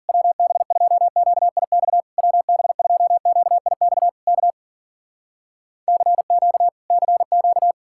Morse code, heard as a keyed tone, about 52 wpm, 700 Hz. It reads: WB2QIX WB2QIX K CQ CQ